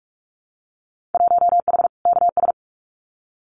Morse code, heard as a keyed tone, about 46 wpm, 700 Hz. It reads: 1H KS
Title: Winter Field Day Exchanges Encore 45wpm